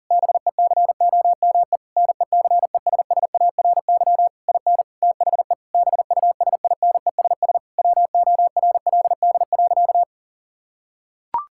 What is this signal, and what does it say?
Morse code, heard as a keyed tone, about 40 wpm, 700 Hz. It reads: BECOME NECESSARY IN THE BUSINESS WORLD.
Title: Sentences from Top 1000 Words - Review Part 5 - 40wpm